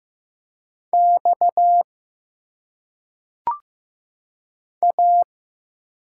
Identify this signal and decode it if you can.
Morse code, heard as a keyed tone, about 15 wpm, 700 Hz. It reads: X A
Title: Single Letters - Phonetic Pronunciation 15wpm